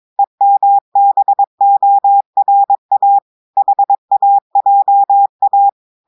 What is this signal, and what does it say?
Morse code, heard as a keyed tone, about 22 wpm, 800 Hz. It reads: EMBORA HAJA
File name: audios / portugais / portugais texte en clair 2 vitesse 22 mots minutes